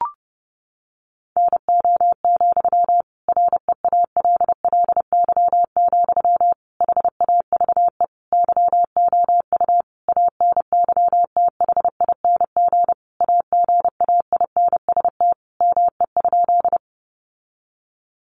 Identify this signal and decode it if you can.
Morse code, heard as a keyed tone, about 30 wpm, 700 Hz. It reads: NO, REALLY, HAVE YOU ANYTHING AGAINST ME?
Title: Sentences from Top 1000 Words 30wpm